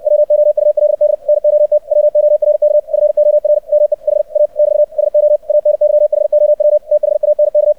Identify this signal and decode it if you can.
Morse code, heard as a keyed tone, about 64 wpm, 600 Hz. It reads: WOULD NOT WORK FOR ME HA <BT> SO ANYHOW THANK
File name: archived / CRSnet